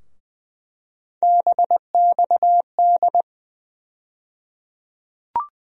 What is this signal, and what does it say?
Morse code, heard as a keyed tone, about 20 wpm, 700 Hz. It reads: BXD
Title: USA Call Sign Postfixes - 3 Letters 20wpm